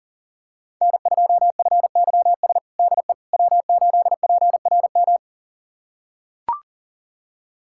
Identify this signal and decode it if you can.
Morse code, heard as a keyed tone, about 40 wpm, 690 Hz. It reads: N2FYS DE W8PRM
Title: USA Callsign DE Callsign - 5 Characters - 40wpm